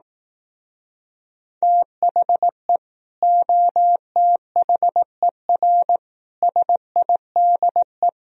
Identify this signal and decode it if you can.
Morse code, heard as a keyed tone, about 18 wpm, 700 Hz. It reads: THE OTHER SIDE
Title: Speed-Racing - Sets of 4 Words 15wpm